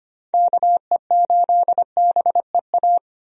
Morse code, heard as a keyed tone, about 25 wpm, 700 Hz. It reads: KE8BEA
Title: USA Call Signs - 6 Characters - 25wpm